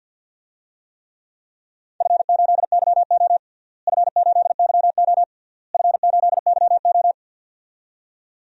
Morse code, heard as a keyed tone, about 50 wpm, 700 Hz. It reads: F8YO F8YO F8YO
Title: DXCC-100 Call Signs - 4 Characters - Repeated 3x - 50wpm